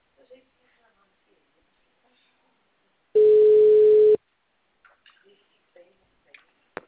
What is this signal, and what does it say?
Ringback tone (Europe, 425 Hz), with speech.